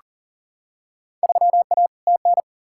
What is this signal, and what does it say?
Morse code, heard as a keyed tone, about 40 wpm, 700 Hz. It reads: 3A TN
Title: Field Day Exchanges - 40wpm